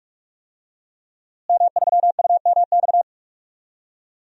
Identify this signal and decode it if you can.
Morse code, heard as a keyed tone, about 46 wpm, 700 Hz. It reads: M3UMX
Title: DXCC-100 Call Signs - 5 Characters - Repeated 3x - 45wpm